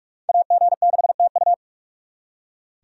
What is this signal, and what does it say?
Morse code, heard as a keyed tone, about 58 wpm, 690 Hz. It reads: AG6TU